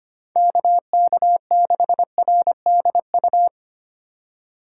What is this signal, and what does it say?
Morse code, heard as a keyed tone, about 27 wpm, 700 Hz. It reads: KK6RDU
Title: Speed Racing - USA 2x3 Call Signs 25wpm